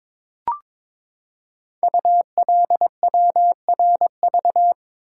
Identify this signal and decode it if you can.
Morse code, heard as a keyed tone, about 23 wpm, 700 Hz. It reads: ULWRV
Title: Code Groups - 5 Letters - 22wpm